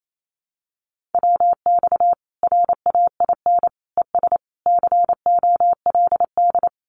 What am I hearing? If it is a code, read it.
Morse code, heard as a keyed tone, about 28 wpm, 700 Hz. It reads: WX RAIN ES COLD
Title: Common QSO Phrases 28wpm